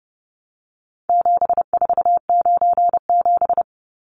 Morse code, heard as a keyed tone, about 30 wpm, 700 Hz. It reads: 7497
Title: Numbers - 4-Digit 30wpm